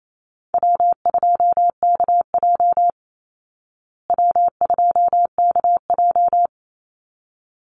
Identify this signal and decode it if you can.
Morse code, heard as a keyed tone, about 28 wpm, 690 Hz. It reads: W2KJ W2KJ
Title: Call Signs Repeated 3x - 28wpm